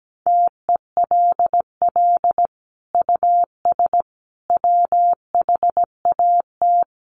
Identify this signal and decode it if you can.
Morse code, heard as a keyed tone, about 17 wpm, 700 Hz. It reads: TELL US WHAT
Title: Sentences from Top 1000 Words 17wpm